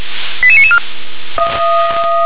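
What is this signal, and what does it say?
Five-tone selective call (ZVEI), heard as quick runs of five tones, tones of about 70 ms. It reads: 88994
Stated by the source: ZVEI five-tone call received on SDR.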